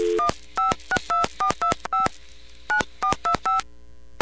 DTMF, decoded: 1562425 9425